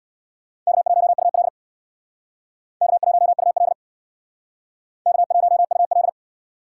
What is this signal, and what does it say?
Morse code, heard as a keyed tone, about 63 wpm, 700 Hz. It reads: K1UL K1UL K1UL